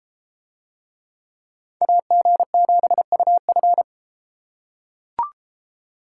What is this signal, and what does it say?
Morse code, heard as a keyed tone, about 30 wpm, 700 Hz. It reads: AG7UF